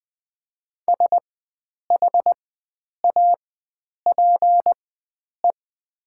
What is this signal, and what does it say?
Morse code, heard as a keyed tone, about 20 wpm, spaced as for 9 wpm, 700 Hz. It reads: SHAPE
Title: 4x Character Spacing - Top 500 Words - 20wpm